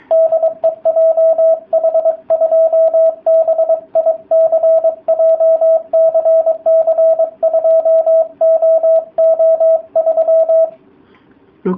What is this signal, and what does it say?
Morse code, heard as a keyed tone, about 23 wpm, 660 Hz, with radio noise. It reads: DE JH2BIC JCC2OO3